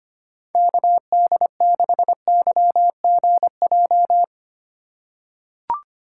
Morse code, heard as a keyed tone, about 25 wpm, 700 Hz. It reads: KD6YGJ
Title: USA Call Signs - 3x Character Spacing - 25wpm